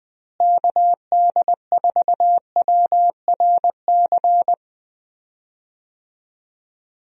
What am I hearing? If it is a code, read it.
Morse code, heard as a keyed tone, about 20 wpm, 700 Hz. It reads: KD4WRC
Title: USA Call Signs - 6 Characters - Repeated 3x - 20wpm